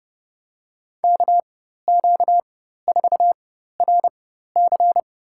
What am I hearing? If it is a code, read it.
Morse code, heard as a keyed tone, about 30 wpm, spaced as for 14 wpm, 700 Hz. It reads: KQ4RC
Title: USA Call Signs - 5 Characters - 4x Character Spacing - 30wpm